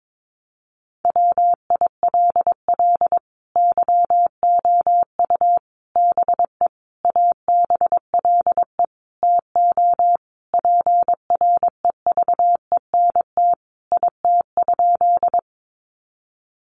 Morse code, heard as a keyed tone, about 22 wpm, 700 Hz. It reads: WILL YOU BE ABLE TO PREVENT IT?
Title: Sentences from Top 1100 Words 22wpm